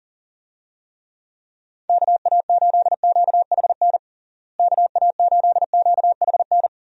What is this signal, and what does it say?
Morse code, heard as a keyed tone, about 40 wpm, 700 Hz. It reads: KA8QHN KA8QHN
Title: USA Call Signs - 6 Characters - Repeated 3x - 40wpm